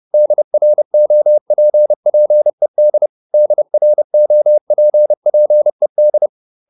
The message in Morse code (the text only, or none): DROPPED DROPPED